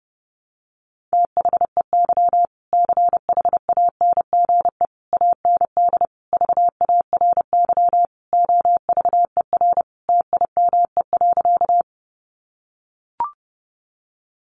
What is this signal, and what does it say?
Morse code, heard as a keyed tone, about 30 wpm, 700 Hz. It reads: THEY CHANGE AND VARY OVER TIME.
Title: Sentences from Top 1200 Words 30wpm